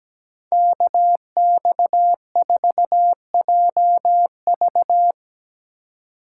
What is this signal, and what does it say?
Morse code, heard as a keyed tone, about 19 wpm, 700 Hz. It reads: KX4JV
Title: Speed Racing - USA 2x2 Call Signs 17wpm